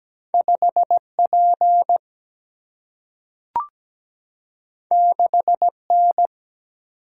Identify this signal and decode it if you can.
Morse code, heard as a keyed tone, about 17 wpm, 700 Hz. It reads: 5P 6N
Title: Single-Letter Single-Number 17wpm